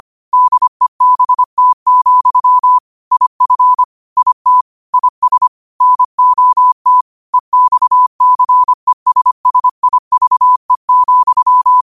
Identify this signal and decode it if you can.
Morse code, heard as a keyed tone, about 25 wpm, 1000 Hz. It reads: DEBT, IF IT IS NOT EXCESSIVE,
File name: qotdfiles / 25wpm